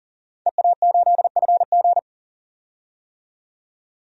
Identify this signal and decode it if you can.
Morse code, heard as a keyed tone, about 39 wpm, 700 Hz. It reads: EA8FG
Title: DXCC-100 Call Signs - 5 Characters - 40wpm